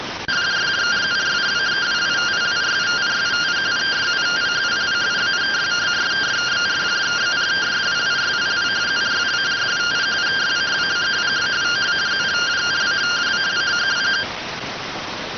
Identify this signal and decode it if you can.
Two-tone teleprinter signal, about 170 Hz shift.